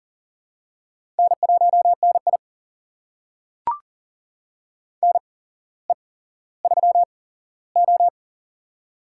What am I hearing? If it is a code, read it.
Morse code, heard as a keyed tone, about 40 wpm, spaced as for 17 wpm, 700 Hz. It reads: N1NI NE3O
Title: USA Call Signs - 4 Characters - 8x Character Spacing - 40wpm